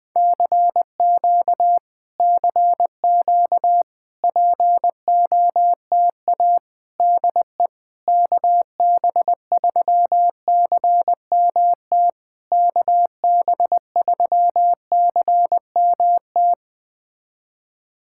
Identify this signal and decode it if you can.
Morse code, heard as a keyed tone, about 20 wpm, 700 Hz. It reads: CQ CQ POTA DE KB3CMT KB3CMT
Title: POTA QSO Practice - 20wpm